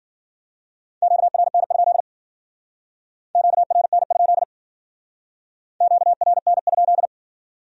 Morse code, heard as a keyed tone, about 54 wpm, 700 Hz. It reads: QRN? QRN? QRN?